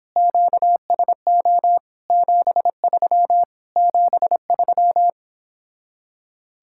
Morse code, heard as a keyed tone, about 31 wpm, 700 Hz. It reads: QSO 73 73